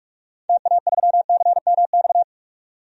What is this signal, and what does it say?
Morse code, heard as a keyed tone, about 42 wpm, 700 Hz. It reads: TA3CMX
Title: DXCC-100 Call Signs - 6 Characters - 45wpm